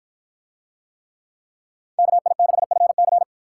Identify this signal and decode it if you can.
Morse code, heard as a keyed tone, about 53 wpm, 700 Hz. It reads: KI6FC